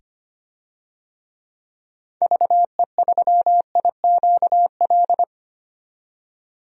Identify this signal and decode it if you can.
Morse code, heard as a keyed tone, about 25 wpm, 700 Hz. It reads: VE3IQL
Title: DXCC-100 Call Signs - 6 Characters - 25wpm